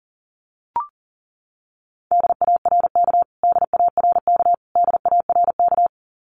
Morse code, heard as a keyed tone, about 40 wpm, 700 Hz. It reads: DARK DARK DARK
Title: Top 500 Words Repeated 40wpm